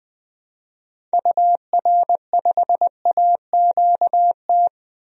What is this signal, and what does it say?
Morse code, heard as a keyed tone, about 20 wpm, 700 Hz. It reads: UR5AQT
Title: DXCC-100 Call Signs - 6 Characters - Repeated 2x - 20wpm